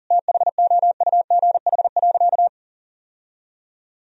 Morse code, heard as a keyed tone, about 40 wpm, 700 Hz. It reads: THOUGH.
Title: Sentences from Top 1000 Words - Review Part 2 - 40wpm